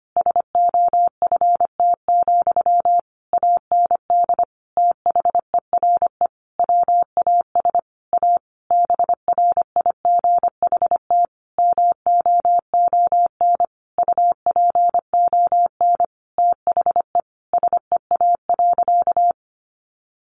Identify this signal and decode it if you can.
Morse code, heard as a keyed tone, about 25 wpm, 700 Hz. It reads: SOFT, AND THERE WAS A BRIGHT MOON UPON THE SEA.
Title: Sentences from Top 1500 Words 25wpm